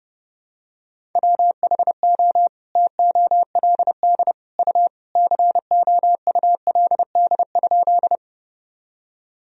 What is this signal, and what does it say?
Morse code, heard as a keyed tone, about 30 wpm, 700 Hz. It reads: WHO TOLD U COULD?
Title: Sentences from Top 400 Words and Abbreviations 30wpm